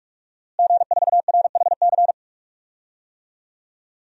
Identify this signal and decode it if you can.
Morse code, heard as a keyed tone, about 41 wpm, 700 Hz. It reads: G4RHC
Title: DXCC-100 Call Signs - 5 Characters - Repeated 3x - 45wpm